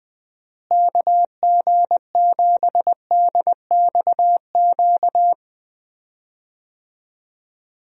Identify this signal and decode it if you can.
Morse code, heard as a keyed tone, about 20 wpm, 700 Hz. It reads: KG7DXQ
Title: USA Call Signs - 6 Characters - Repeated 2x - 20wpm